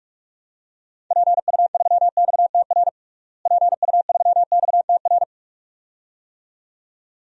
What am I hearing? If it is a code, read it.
Morse code, heard as a keyed tone, about 42 wpm, 700 Hz. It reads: PU3XTR PU3XTR
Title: DXCC-100 Call Signs - 6 Characters - Repeated 2x - 45wpm